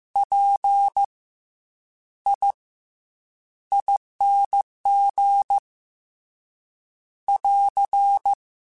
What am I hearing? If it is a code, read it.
Morse code, heard as a keyed tone, about 15 wpm, spaced as for 8 wpm, 800 Hz. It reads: PIING <AR>